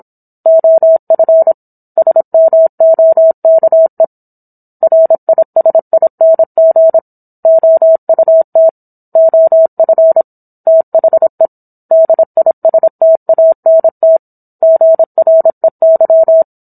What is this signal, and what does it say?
Morse code, heard as a keyed tone, about 27 wpm, 650 Hz. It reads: OF SMOKE RISING OUT OF THE DISTANT GREY